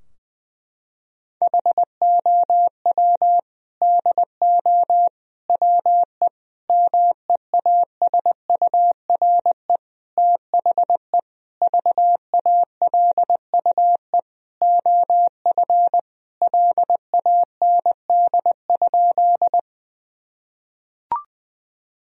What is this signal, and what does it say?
Morse code, heard as a keyed tone, about 20 wpm, 700 Hz. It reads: HOW DO WE MEASURE THE VALUE OF LAND?
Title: Sentences from Top 1000 Words - Review Part 3 - 20wpm